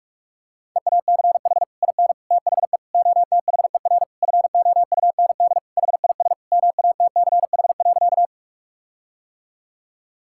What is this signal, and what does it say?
Morse code, heard as a keyed tone, about 41 wpm, 700 Hz. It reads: EACH IN THE OTHER FOUND HIS MATCH.
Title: Sentences from Top 1600 Words 45wpm